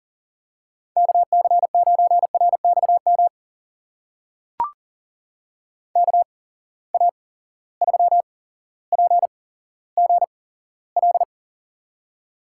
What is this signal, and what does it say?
Morse code, heard as a keyed tone, about 40 wpm, spaced as for 17 wpm, 700 Hz. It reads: KC9RXM KA3PGL
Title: USA Call Signs - 6 Characters - 8x Character Spacing - 40wpm